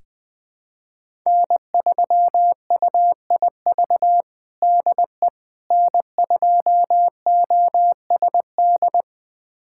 Morse code, heard as a keyed tone, about 20 wpm, 700 Hz. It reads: N3UIV DE N2OSD
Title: USA Callsign DE Callsign - 5 Characters - 20wpm